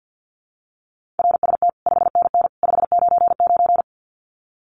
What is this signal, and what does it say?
Morse code, heard as a keyed tone, about 50 wpm, 700 Hz. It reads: RST 5NN 599
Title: Common QSO Phrases 50wpm